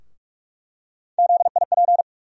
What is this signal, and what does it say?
Morse code, heard as a keyed tone, about 46 wpm, 700 Hz. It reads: ZIP
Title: USA Call Sign Postfixes - 3 Letters 45wpm